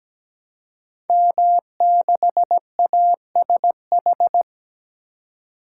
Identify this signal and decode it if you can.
Morse code, heard as a keyed tone, about 17 wpm, 700 Hz. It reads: M6ASH
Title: Code Groups - 5 Letter:Numbers - 17wpm